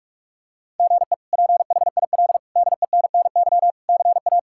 Morse code, heard as a keyed tone, about 46 wpm, 700 Hz. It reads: GE PHIL DENNY CA
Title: K1USN SST QSO Practice - 45wpm